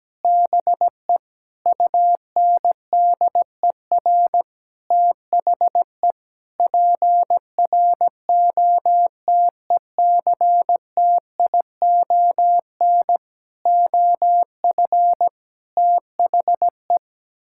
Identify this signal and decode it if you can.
Morse code, heard as a keyed tone, about 17 wpm, 700 Hz. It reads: BE UNDER THE PROTECTION OF THE
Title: Sentences from Top 1600 Words 17wpm